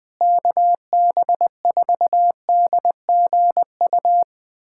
Morse code, heard as a keyed tone, about 20 wpm, 700 Hz. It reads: KB4DGU
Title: USA Callsign DE Callsign - 6 Characters - 20wpm